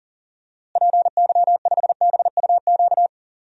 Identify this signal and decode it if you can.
Morse code, heard as a keyed tone, about 40 wpm, 700 Hz. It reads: PY5BUQ
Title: DXCC-100 Call Signs - 6 Characters - Repeated 3x - 40wpm